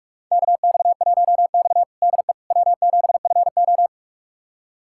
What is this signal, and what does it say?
Morse code, heard as a keyed tone, about 44 wpm, 700 Hz. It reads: KX1X DE W7FO
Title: USA Callsign DE Callsign - 4 Characters - 45wpm